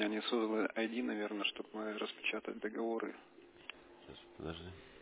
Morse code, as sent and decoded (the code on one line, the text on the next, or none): none